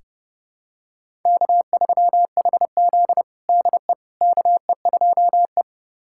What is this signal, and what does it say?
Morse code, heard as a keyed tone, about 30 wpm, 700 Hz. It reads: K3HZ DE KE2E
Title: USA Callsign DE Callsign - 4 Characters - 30wpm